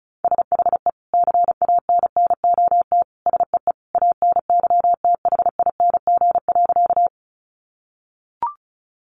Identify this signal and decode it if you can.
Morse code, heard as a keyed tone, about 34 wpm, 700 Hz. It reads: SHE CANNOT SEE ANYTHING.
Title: Additional Easy Phrases 35wpm